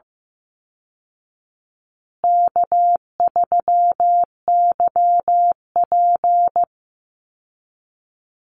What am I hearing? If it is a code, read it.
Morse code, heard as a keyed tone, about 15 wpm, 700 Hz. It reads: K3YP
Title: Call Signs Repeated 2x - 15wpm